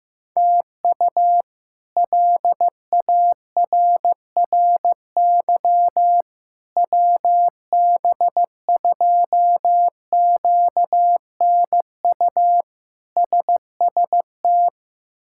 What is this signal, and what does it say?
Morse code, heard as a keyed tone, about 15 wpm, 700 Hz. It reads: TU LARRY WB2QNU SST
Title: K1USN SST QSO Practice - 15wpm